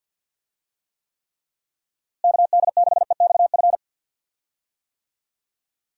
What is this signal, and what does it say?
Morse code, heard as a keyed tone, about 50 wpm, 700 Hz. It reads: KD6EXF
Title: USA Call Signs - 6 Characters - Repeated 3x - 50wpm